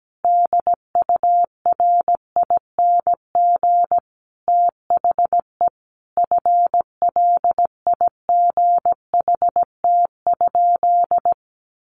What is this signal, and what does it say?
Morse code, heard as a keyed tone, about 17 wpm, 700 Hz. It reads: DURING THE FLIGHT?
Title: Sentences from Top 1000 Words 17wpm